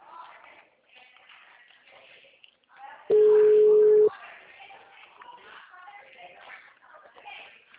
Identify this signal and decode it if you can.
Ringback tone (Europe, 425 Hz), with speech.
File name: out- -2017- - - .58